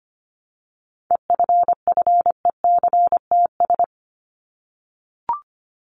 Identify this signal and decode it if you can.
Morse code, heard as a keyed tone, about 25 wpm, 700 Hz. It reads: EFFECTS
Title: ES:S Suffix - Top 100 Words with Base Words 25wpm